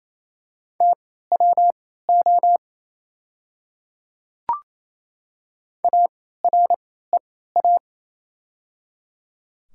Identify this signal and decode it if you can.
Morse code, heard as a keyed tone, about 28 wpm, spaced as for 16 wpm, 690 Hz. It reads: TWO AREA
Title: 3x Character Spacing - Top 500 Words - 28wpm